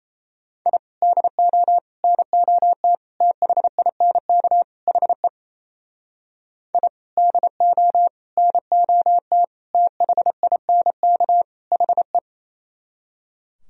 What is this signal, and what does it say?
Morse code, heard as a keyed tone, about 38 wpm, 700 Hz. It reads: I DO NOT THINK HE I DO NOT THINK HE